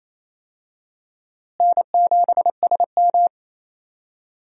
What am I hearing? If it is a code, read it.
Morse code, heard as a keyed tone, about 28 wpm, 700 Hz. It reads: N7SM
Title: USA Call Signs - 4 Characters - Repeated 2x - 28wpm